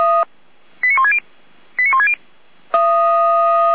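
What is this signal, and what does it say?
Five-tone selective call (ZVEI), heard as quick runs of five tones, tones of about 70 ms. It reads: 89160 89160
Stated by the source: ZVEI five-tone selective call received on an SDR.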